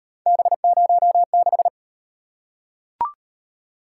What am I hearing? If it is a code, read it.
Morse code, heard as a keyed tone, about 35 wpm, 700 Hz. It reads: B06